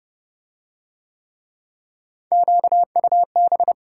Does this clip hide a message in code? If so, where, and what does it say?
Yes: Morse code from 2.3 to 3.7 s, heard as a keyed tone, about 25 wpm, 700 Hz. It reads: QUB